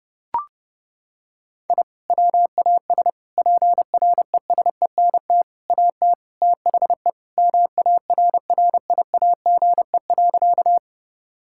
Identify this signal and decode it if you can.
Morse code, heard as a keyed tone, about 32 wpm, 700 Hz. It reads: I WAS PRESENT AT THE MARRIAGE.
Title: Sentences from Top 1000 Words - Review Part 5 - 30wpm